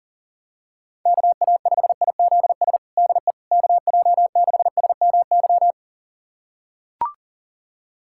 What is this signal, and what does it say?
Morse code, heard as a keyed tone, about 37 wpm, 700 Hz. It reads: KA5IZS DE KJ6SMY
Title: USA Callsign DE Callsign - 6 Characters - 40wpm